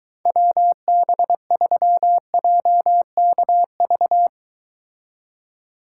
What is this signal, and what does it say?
Morse code, heard as a keyed tone, about 23 wpm, 700 Hz. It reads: WB3JKV